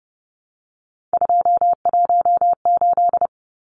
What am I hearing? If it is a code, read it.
Morse code, heard as a keyed tone, about 30 wpm, 700 Hz. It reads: 218
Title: Numbers - 3-Digit 30wpm